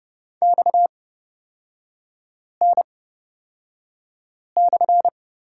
Morse code, heard as a keyed tone, about 29 wpm, spaced as for 5 wpm, 700 Hz. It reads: XN/
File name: Lesson 37e - Letter x - Three Characters 30wpm